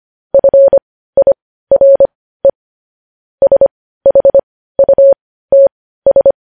Morse code, heard as a keyed tone, about 25 wpm, spaced as for 15 wpm, 550 Hz. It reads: FIRE SHUTS
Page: news headlines in morse code at 15 wpm